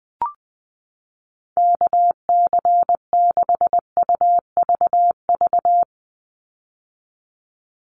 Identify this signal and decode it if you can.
Morse code, heard as a keyed tone, about 20 wpm, 700 Hz. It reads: KC6UVV
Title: USA Call Signs 20wpm